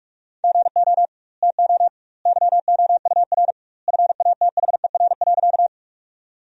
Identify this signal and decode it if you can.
Morse code, heard as a keyed tone, about 44 wpm, 700 Hz. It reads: GO TO YOUR FATHER.
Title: 2x Word-Spacing - Sentences from Top 300 Words - 45wpm